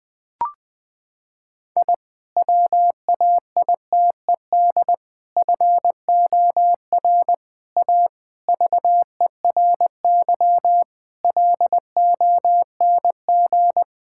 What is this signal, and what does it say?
Morse code, heard as a keyed tone, about 20 wpm, 700 Hz. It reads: I WAITED FOR A VERY LONG
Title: Sentences with 5-Word Phrases - 20wpm